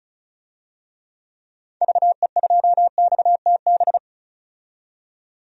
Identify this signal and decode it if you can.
Morse code, heard as a keyed tone, about 36 wpm, 700 Hz. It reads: VE2XTB
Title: DXCC-100 Call Signs - 6 Characters - 35wpm